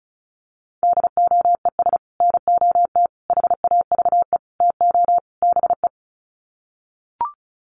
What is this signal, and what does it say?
Morse code, heard as a keyed tone, about 34 wpm, 700 Hz. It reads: DOES NOT HAVE TO BE
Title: Sets of 5 Words 35wpm